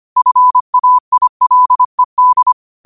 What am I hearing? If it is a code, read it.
Morse code, heard as a keyed tone, about 25 wpm, 1000 Hz. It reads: FAILED
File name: qotdfiles / 25wpm